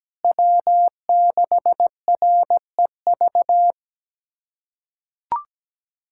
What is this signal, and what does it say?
Morse code, heard as a keyed tone, about 17 wpm, 700 Hz. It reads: W6REV
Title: USA Callsign DE Callsign 17wpm